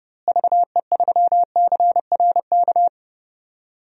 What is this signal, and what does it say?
Morse code, heard as a keyed tone, about 30 wpm, 700 Hz. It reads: VE3CRK
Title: DXCC-100 Call Signs - 6 Characters - Repeated 3x - 30wpm